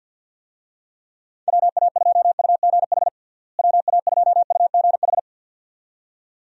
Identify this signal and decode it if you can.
Morse code, heard as a keyed tone, about 51 wpm, 700 Hz. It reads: WA2UGH WA2UGH
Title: USA Call Signs - 6 Characters - Repeated 2x - 50wpm